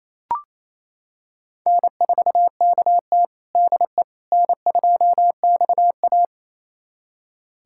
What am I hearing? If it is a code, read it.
Morse code, heard as a keyed tone, about 28 wpm, 700 Hz. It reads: N4KT DE N2XA
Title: USA Callsign DE Callsign - 4 Characters - 28wpm